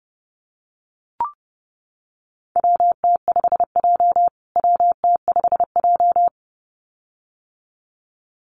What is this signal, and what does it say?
Morse code, heard as a keyed tone, about 30 wpm, 700 Hz. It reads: WT5J WT5J
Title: USA Call Signs Repeated 2x 30wpm